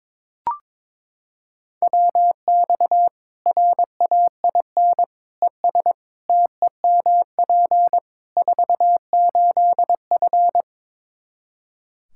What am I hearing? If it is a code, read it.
Morse code, heard as a keyed tone, about 22 wpm, 700 Hz. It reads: WX RAIN ES TEMP 48F
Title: Practice QSO - Exchange 2 - Info - 22wpm